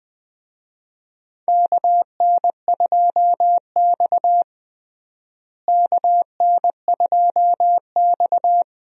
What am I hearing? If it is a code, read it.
Morse code, heard as a keyed tone, about 20 wpm, 700 Hz. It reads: KN2X KN2X
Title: USA Call Signs - 4 Characters - Repeated 3x - 20wpm